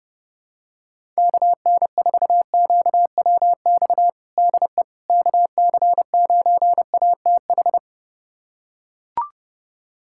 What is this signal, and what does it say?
Morse code, heard as a keyed tone, about 30 wpm, 700 Hz. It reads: KN4QWX DE KC9ATH
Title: USA Callsign DE Callsign - 6 Characters - 30wpm